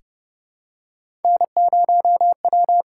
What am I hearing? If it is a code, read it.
Morse code, heard as a keyed tone, about 30 wpm, 700 Hz. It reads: N0W
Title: USA Call Signs - 8x Character Spacing - 30wpm